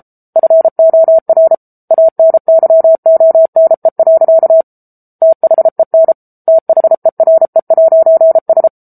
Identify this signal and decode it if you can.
Morse code, heard as a keyed tone, about 34 wpm, 650 Hz. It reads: FOR ANYONE. THEN THERE'S